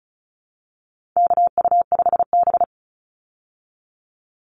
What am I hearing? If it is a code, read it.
Morse code, heard as a keyed tone, about 34 wpm, 700 Hz. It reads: KU5B
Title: Call Signs Repeated 2x - 35wpm